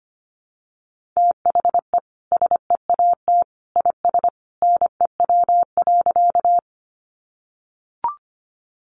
Morse code, heard as a keyed tone, about 25 wpm, 700 Hz. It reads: THE SEAT IS NEW.
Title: Sentences from Top 1500 Words 25wpm